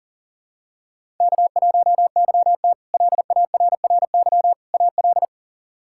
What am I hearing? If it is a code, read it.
Morse code, heard as a keyed tone, about 40 wpm, 700 Hz. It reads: K1YT LARRY AL
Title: K1USN SST QSO Practice - 40wpm